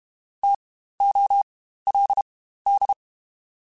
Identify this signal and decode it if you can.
Morse code, heard as a keyed tone, about 32 wpm, spaced as for 15 wpm, 800 Hz. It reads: TOLD
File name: Spaced Alphas / Spaced Alphas mp3s / Ts sp @30